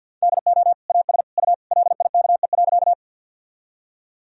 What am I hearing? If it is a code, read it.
Morse code, heard as a keyed tone, about 53 wpm, 700 Hz. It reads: DO AS U LIKE.
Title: Sentences from Top 100 Words and Abbreviations 50wpm